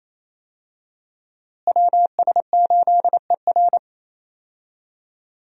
Morse code, heard as a keyed tone, about 28 wpm, 700 Hz. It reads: WS8ER
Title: USA Call Signs - 5 Characters - Repeated 3x - 28wpm